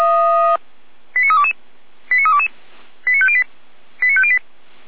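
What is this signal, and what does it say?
Five-tone selective call (ZVEI), heard as quick runs of five tones, tones of about 70 ms. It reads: 89322 89322 89508 89508
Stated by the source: ZVEI five-tone call received by SDR.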